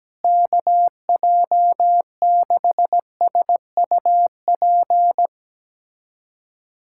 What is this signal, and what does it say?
Morse code, heard as a keyed tone, about 17 wpm, 700 Hz. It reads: KJ6SUP
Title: USA Callsign DE Callsign 17wpm